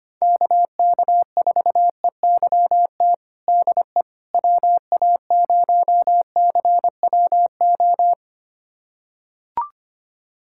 Morse code, heard as a keyed tone, about 25 wpm, 700 Hz. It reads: KK4EYT DE WA0CWO
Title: USA Callsign DE Callsign - 6 Characters - 25wpm